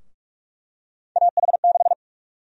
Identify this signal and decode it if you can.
Morse code, heard as a keyed tone, about 46 wpm, 700 Hz. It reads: AH6
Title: USA Call Sign Prefixes - Repeated 2x - 45wpm